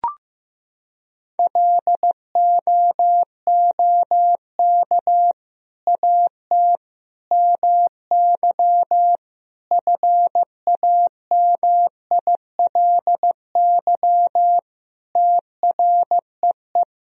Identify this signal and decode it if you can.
Morse code, heard as a keyed tone, about 16 wpm, 700 Hz. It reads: LOOK AT MY FAMILY TREE
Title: Sentences from Top 1000 Words - Review Part 1 - 15wpm